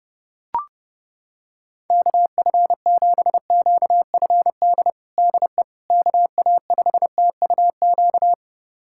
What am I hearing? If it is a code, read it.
Morse code, heard as a keyed tone, about 30 wpm, 700 Hz. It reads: KF7QFD DE KA5TUQ
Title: USA Callsign DE Callsign - 6 Characters - 30wpm